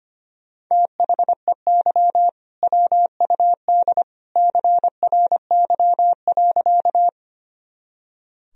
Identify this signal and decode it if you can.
Morse code, heard as a keyed tone, about 25 wpm, 700 Hz. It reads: THEY WUD CRY.
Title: Sentences from Top 500 Words and Abbreviations 25wpm